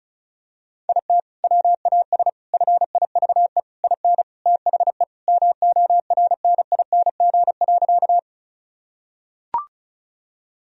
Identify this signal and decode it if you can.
Morse code, heard as a keyed tone, about 34 wpm, 700 Hz. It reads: IT WAS FIVE IN THE MORNING.
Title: Sentences from Top 1000 Words - Review Part 2 - 35wpm